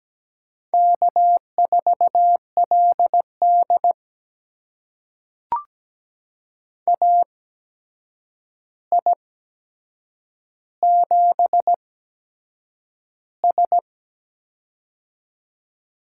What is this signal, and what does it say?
Morse code, heard as a keyed tone, about 17 wpm, spaced as for 7 wpm, 700 Hz. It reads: K4LD AI7S
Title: USA Call Signs - 4 Characters - 8x Character Spacing - 17wpm